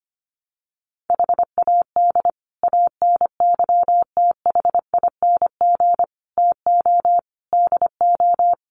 Morse code, heard as a keyed tone, about 25 wpm, 700 Hz. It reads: HAD ANYTHING TO DO
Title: Sets of 5 Words Encore 25wpm